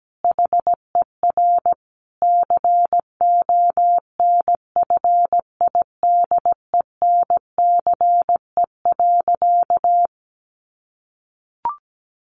Morse code, heard as a keyed tone, about 17 wpm, 700 Hz. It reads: HER CONFIDENCE.
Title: Sentences from Top 1500 Words 17wpm